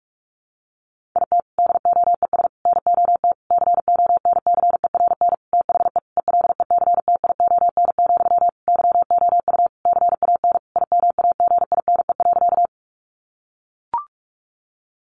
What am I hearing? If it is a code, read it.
Morse code, heard as a keyed tone, about 44 wpm, 700 Hz. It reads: IT DOES NOT CONCERN THE ELECTION, YOU CAN IMAGINE.
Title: Sentences from Top 1500 Words 45wpm